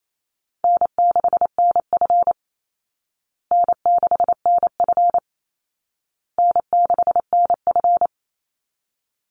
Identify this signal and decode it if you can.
Morse code, heard as a keyed tone, about 28 wpm, 700 Hz. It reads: N6NF N6NF N6NF
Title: Easy Call Signs Repeated 3x - 28wpm